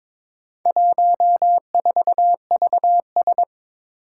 Morse code, heard as a keyed tone, about 23 wpm, 700 Hz. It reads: 14VS
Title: Four Challenging Letter-Number Combinations 22wpm